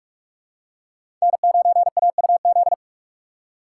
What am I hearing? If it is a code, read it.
Morse code, heard as a keyed tone, about 42 wpm, 700 Hz. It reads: N9AUZ
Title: USA Call Signs - 4x Character Spacing - 45wpm